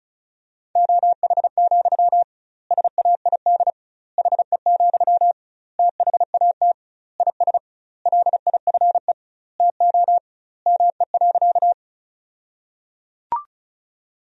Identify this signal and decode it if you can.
Morse code, heard as a keyed tone, about 34 wpm, 700 Hz. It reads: OH, SAID HE, THAT IS LIFE TO ME.
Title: 2x Word-Spacing - Sentences from Top 500 Words - 35wpm